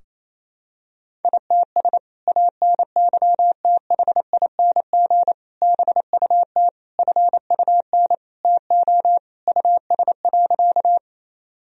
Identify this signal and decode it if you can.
Morse code, heard as a keyed tone, about 28 wpm, 700 Hz. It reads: ITS ANYTHING BUT FUN TO US.
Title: Sentences from Top 1000 Words - Review Part 4 - 28wpm